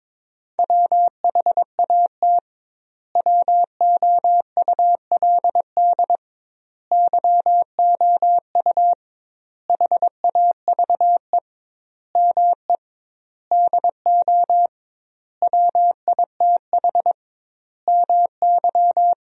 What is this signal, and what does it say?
Morse code, heard as a keyed tone, about 22 wpm, 700 Hz. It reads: WHAT WOULD YOU HAVE ME DO WITH MY
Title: 2x Word-Spacing - Sentences from Top 300 Words - 22wpm